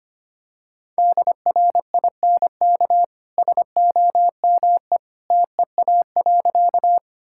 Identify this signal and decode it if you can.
Morse code, heard as a keyed tone, about 25 wpm, 700 Hz. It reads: DRINK SOME TEA.
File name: Easy Sentences with Hints - 25wpm